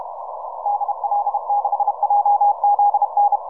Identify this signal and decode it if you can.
Morse code, heard as a keyed tone, about 32 wpm, 820 Hz. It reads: DL6JZN